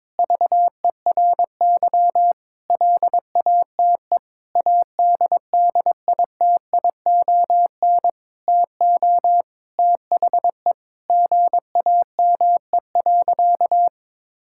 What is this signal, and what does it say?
Morse code, heard as a keyed tone, about 21 wpm, 700 Hz. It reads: VERY LATE ADDITION TO THE GAME.
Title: Sentences from Top 1000 Words - Review Part 6 - 22wpm